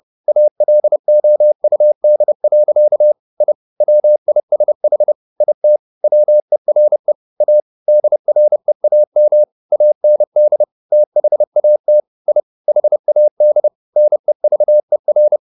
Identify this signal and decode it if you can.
Morse code, heard as a keyed tone, about 30 wpm, 600 Hz. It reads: ALOUD. I WISH IT WERE A DREAM AND THAT I HAD NEVER